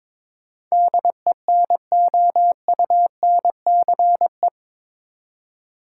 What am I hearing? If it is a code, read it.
Morse code, heard as a keyed tone, about 22 wpm, 700 Hz. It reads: DENOUNCE
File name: practice / 22wpm / Eight Letter Words - 22wpm